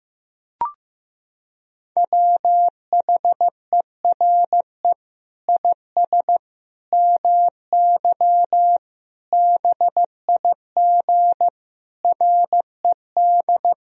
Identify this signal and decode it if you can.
Morse code, heard as a keyed tone, about 16 wpm, 700 Hz. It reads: WHERE IS MY BIG RED
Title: Sentences from Top 1000 Words - Review Part 2 - 15wpm